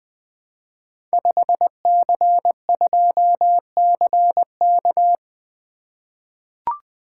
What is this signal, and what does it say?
Morse code, heard as a keyed tone, about 21 wpm, 700 Hz. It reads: 5C2CK
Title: Code Groups - 5 Letter:Numbers - 20wpm